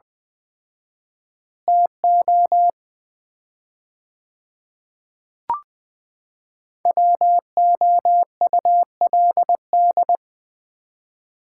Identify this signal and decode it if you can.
Morse code, heard as a keyed tone, about 20 wpm, 700 Hz. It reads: TO WOULD
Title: 4x Word-Spacing - Sets of 4 Words - 20wpm